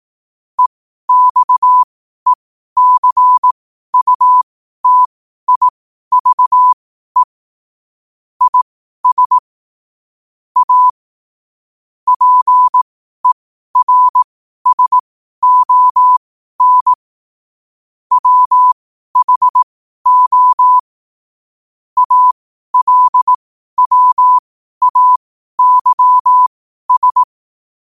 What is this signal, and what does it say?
Morse code, heard as a keyed tone, about 18 wpm, spaced as for 12 wpm, 1000 Hz. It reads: EXECUTIVE IS A PERSON WHO ALWAYS
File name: qotdfiles / 12wpm